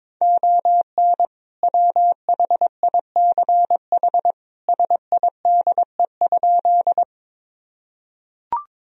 Morse code, heard as a keyed tone, about 22 wpm, 700 Hz. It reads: ON WHICH SIDE?
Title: Sentences from Top 1000 Words - Review Part 1 - 22wpm